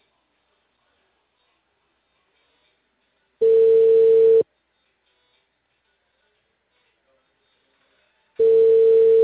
Ringback tone (China, 450 Hz).